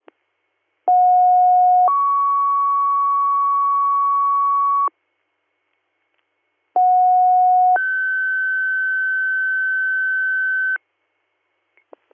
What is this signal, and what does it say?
Two-tone page, heard as a one-second tone, then a three-second tone, about 730 then 1120 Hz, about 730 then 1550 Hz. It is heard from 0.9 to 10.8 s.